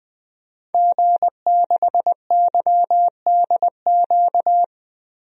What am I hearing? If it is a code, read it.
Morse code, heard as a keyed tone, about 21 wpm, 700 Hz. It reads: G6YDQ
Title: DXCC-100 Call Signs - 5 Characters - Repeated 2x - 20wpm